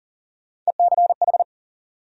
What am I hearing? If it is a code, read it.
Morse code, heard as a keyed tone, about 40 wpm, 700 Hz. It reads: ECH
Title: USA Call Sign Postfixes - 3 Letters 40wpm